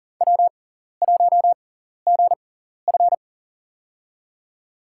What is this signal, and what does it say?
Morse code, heard as a keyed tone, about 40 wpm, spaced as for 14 wpm, 700 Hz. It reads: W1GF
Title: USA Call Signs - 4 Characters - 6x Character Spacing - 40wpm